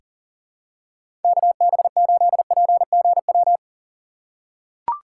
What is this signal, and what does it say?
Morse code, heard as a keyed tone, about 40 wpm, 700 Hz. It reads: KB8PGW
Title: K1USN SST QSO Practice - 40wpm